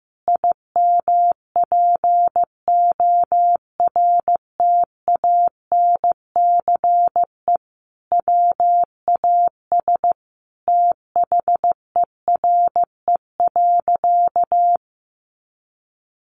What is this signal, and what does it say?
Morse code, heard as a keyed tone, about 15 wpm, 700 Hz. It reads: IMPORTANCE WAS THERE.
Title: Sentences from Top 1400 Words 15wpm